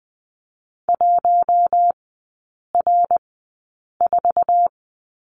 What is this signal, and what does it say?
Morse code, heard as a keyed tone, about 20 wpm, spaced as for 8 wpm, 700 Hz. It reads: 1R4
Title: 053 Lesson 10e - Letter r - Three Characters 20wpm